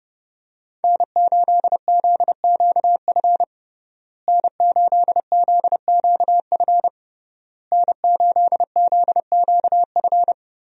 Morse code, heard as a keyed tone, about 30 wpm, 700 Hz. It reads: N8ZQF N8ZQF N8ZQF
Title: USA Call Signs - 5 Characters - Repeated 3x - 30wpm